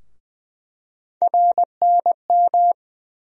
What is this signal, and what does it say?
Morse code, heard as a keyed tone, about 20 wpm, 700 Hz. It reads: RNM
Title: USA Call Sign Postfixes - 3 Letters 20wpm